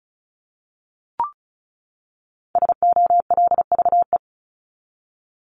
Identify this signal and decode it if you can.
Morse code, heard as a keyed tone, about 34 wpm, 700 Hz. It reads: SOLVE
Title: Top 751-1000 Words 35wpm